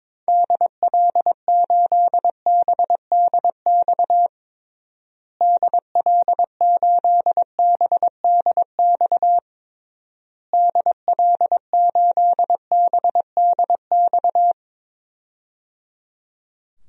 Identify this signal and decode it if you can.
Morse code, heard as a keyed tone, about 22 wpm, 700 Hz. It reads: DL8BDX DL8BDX DL8BDX
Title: DXCC-100 Call Signs - 6 Characters - Repeated 3x - 22wpm